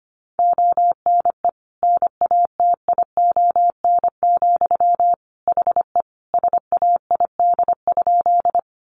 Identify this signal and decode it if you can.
Morse code, heard as a keyed tone, about 25 wpm, 700 Hz. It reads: ONE NATION, HE SAID?
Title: Sentences from Top 900 Words 25wpm